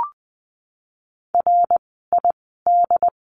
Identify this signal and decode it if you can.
Morse code, heard as a keyed tone, about 20 wpm, spaced as for 14 wpm, 700 Hz. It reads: RID
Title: 068 Lesson 12f - Letter d - Words 20wpm